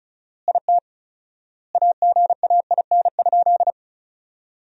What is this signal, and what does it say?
Morse code, heard as a keyed tone, about 36 wpm, 700 Hz. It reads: IT AGAIN?
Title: 4x Word-Spacing - Sentences from Top 200 Words - 35wpm